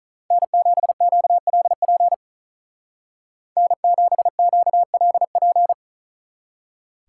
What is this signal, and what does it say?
Morse code, heard as a keyed tone, about 43 wpm, 700 Hz. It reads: N7QLP N7QLP
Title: Speed Racing - USA 1x3 Call Signs 35wpm